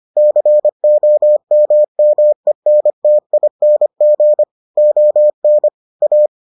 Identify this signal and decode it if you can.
Morse code, heard as a keyed tone, about 25 wpm, 600 Hz. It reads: COMMENTING ON A